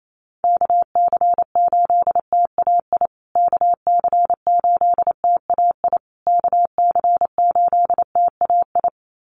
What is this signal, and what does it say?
Morse code, heard as a keyed tone, about 28 wpm, 700 Hz. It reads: KC8TAI KC8TAI KC8TAI
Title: USA Call Signs Repeated 3x 28wpm